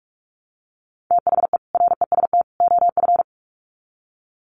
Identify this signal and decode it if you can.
Morse code, heard as a keyed tone, about 44 wpm, 700 Hz. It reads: THE REST OF
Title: Sets of 3 Words 45wpm